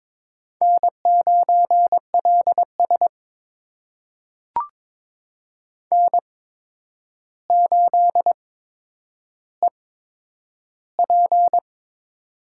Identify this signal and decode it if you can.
Morse code, heard as a keyed tone, about 21 wpm, spaced as for 9 wpm, 700 Hz. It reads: N9LS N8EP
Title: USA Call Signs - 8x Character Spacing - 22wpm